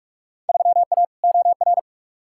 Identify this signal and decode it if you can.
Morse code, heard as a keyed tone, about 41 wpm, 700 Hz. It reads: 3A OR
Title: Field Day Exchanges Encore - 45wpm